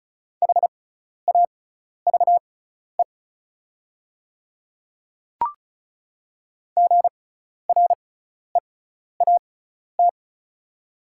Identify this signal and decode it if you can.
Morse code, heard as a keyed tone, about 36 wpm, spaced as for 12 wpm, 700 Hz. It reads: HAVE GREAT
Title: 6x Character Spacing - Top 200 Words - 35wpm